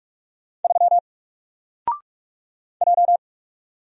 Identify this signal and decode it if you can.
Morse code, heard as a keyed tone, about 44 wpm, 700 Hz. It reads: 3 J
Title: Single Letter-Number Alt 45wpm